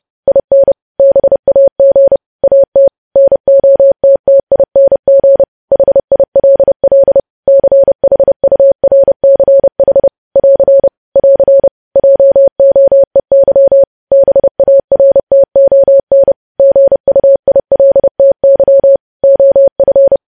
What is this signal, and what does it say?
Morse code, heard as a keyed tone, about 30 wpm, 550 Hz. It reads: IN BAG AT NOTTING HILL CHURCH <AR> <AR> JOEY BARTON GUILTY OF
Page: news headlines in morse code at 30 wpm